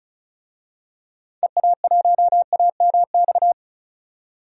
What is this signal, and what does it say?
Morse code, heard as a keyed tone, about 34 wpm, 700 Hz. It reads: EA1AMX
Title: DXCC-100 Call Signs - 6 Characters - 35wpm